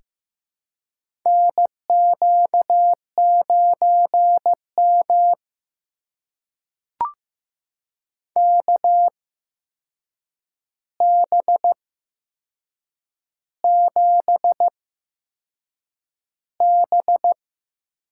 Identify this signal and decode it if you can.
Morse code, heard as a keyed tone, about 15 wpm, spaced as for 6 wpm, 700 Hz. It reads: NQ9M KB7B
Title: USA Call Signs - 8x Character Spacing - 15wpm